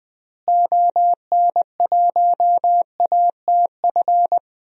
Morse code, heard as a keyed tone, about 20 wpm, 700 Hz. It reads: ON1ATF
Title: DXCC-100 Call Signs - 6 Characters - Repeated 3x - 20wpm